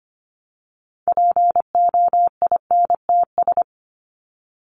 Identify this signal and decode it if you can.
Morse code, heard as a keyed tone, about 25 wpm, 700 Hz. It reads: POINTS
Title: ES:S Suffix - Top 100 Words with Base Words 25wpm